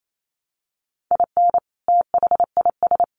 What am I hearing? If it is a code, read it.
Morse code, heard as a keyed tone, about 28 wpm, 700 Hz. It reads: IN THIS
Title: Sets of 3 Words 28wpm